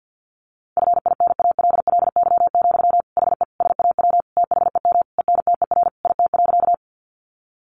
Morse code, heard as a keyed tone, about 50 wpm, 700 Hz. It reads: FINALLY, HE SAW THEM ENTER IT.
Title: Sentences from Top 1300 Words 50wpm